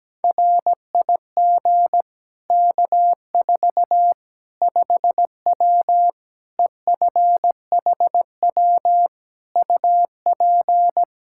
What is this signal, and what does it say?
Morse code, heard as a keyed tone, about 17 wpm, 700 Hz. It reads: RIG K4 5W EFHW UP
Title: Practice QSO - Exchange 2 - Info - 17wpm